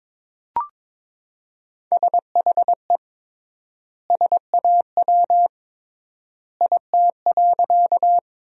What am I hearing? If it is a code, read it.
Morse code, heard as a keyed tone, about 22 wpm, 700 Hz. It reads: SHE SAW IT.
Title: 3x Word-Spacing - Sentences from Top 300 Words - 22wpm